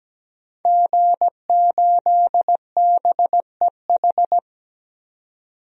Morse code, heard as a keyed tone, about 17 wpm, 700 Hz. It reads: G8BEH
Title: DXCC-100 Call Signs - 5 Characters - 17wpm